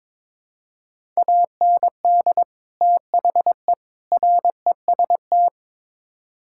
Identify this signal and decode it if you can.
Morse code, heard as a keyed tone, about 22 wpm, 700 Hz. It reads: AND THE REST
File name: practice / 22wpm / Build A Phrase - 5-Word Phrases - 22wpm